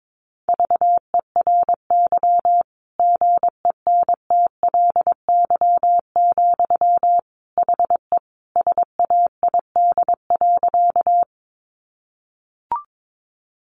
Morse code, heard as a keyed tone, about 22 wpm, 700 Hz. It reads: VERY GENTLY, HE SAID.
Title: Sentences from Top 1500 Words 22wpm